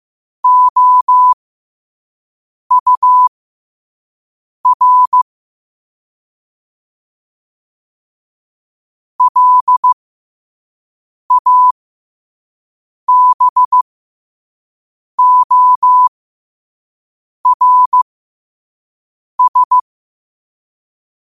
Morse code, heard as a keyed tone, about 15 wpm, spaced as for 5 wpm, 1000 Hz. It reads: OUR LABORS